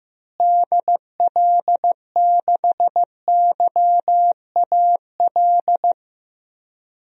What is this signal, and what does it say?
Morse code, heard as a keyed tone, about 16 wpm, 700 Hz. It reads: DL6YAL
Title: DXCC-100 Call Signs - 6 Characters - Repeated 3x - 15wpm